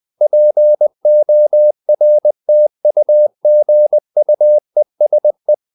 Morse code, heard as a keyed tone, about 20 wpm, 600 Hz. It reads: PORTUGUESE
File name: the old man and the sea / the old man and the sea part 1 20wpm